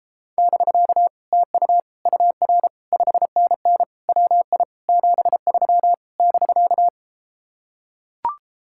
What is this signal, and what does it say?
Morse code, heard as a keyed tone, about 34 wpm, 700 Hz. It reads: <BK> TU UR 5NN WI 73 <BK>